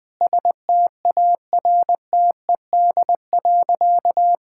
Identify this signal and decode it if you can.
Morse code, heard as a keyed tone, about 20 wpm, 700 Hz. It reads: STARTED.
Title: Sentences with 5-Word Phrases - Spoken Beginning - 20wpm